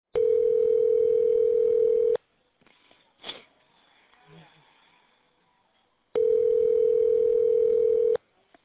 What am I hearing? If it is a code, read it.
Ringback tone (North America), 440 + 480 Hz.